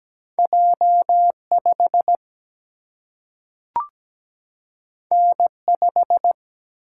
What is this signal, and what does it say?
Morse code, heard as a keyed tone, about 17 wpm, 700 Hz. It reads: J5 N5
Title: Single-Letter Single-Number 17wpm